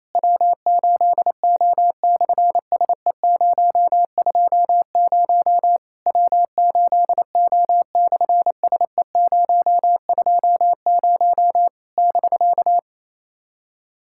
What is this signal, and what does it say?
Morse code, heard as a keyed tone, about 29 wpm, 700 Hz. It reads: W8O/SE020 W8O/SE020 <BK>
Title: SOTA QSO Practice 28wpm